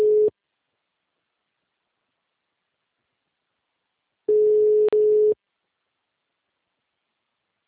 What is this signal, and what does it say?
Ringback tone (Europe, 425 Hz).
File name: out- -2001- - - .70084